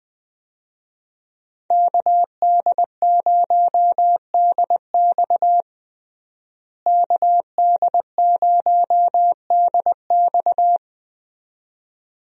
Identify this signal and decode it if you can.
Morse code, heard as a keyed tone, about 20 wpm, 700 Hz. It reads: KD0DX KD0DX
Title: USA Call Signs - 5 Characters - Repeated 2x - 20wpm